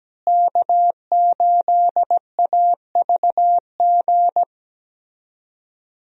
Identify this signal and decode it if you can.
Morse code, heard as a keyed tone, about 17 wpm, 700 Hz. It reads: K8AVG
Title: USA Call Signs - 5 Characters - Repeated 3x - 17wpm